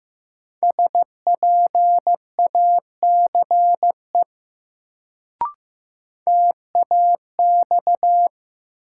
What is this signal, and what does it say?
Morse code, heard as a keyed tone, about 15 wpm, 700 Hz. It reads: SPACE TAX
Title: Sentences from Top 1000 Words - Review Part 5 - New Word Drill - 15wpm